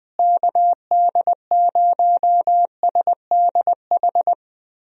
Morse code, heard as a keyed tone, about 21 wpm, 700 Hz. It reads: KD0SDH
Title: USA Call Signs - 6 Characters - Repeated 3x - 20wpm